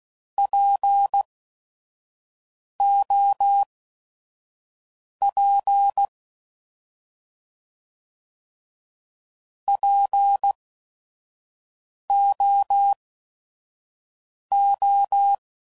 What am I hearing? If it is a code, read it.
Morse code, heard as a keyed tone, about 16 wpm, spaced as for 5 wpm, 800 Hz. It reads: POP POO
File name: Prefixes / By First Letter / Ps